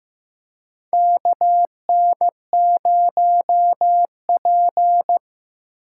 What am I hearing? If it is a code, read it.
Morse code, heard as a keyed tone, about 15 wpm, 700 Hz. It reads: KN0P
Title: USA Call Signs - 4 Characters - 4x Character Spacing - 15wpm